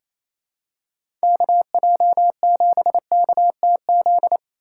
Morse code, heard as a keyed tone, about 28 wpm, 700 Hz. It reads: KJ7KTZ
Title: USA Call Signs - 8x Character Spacing - 28wpm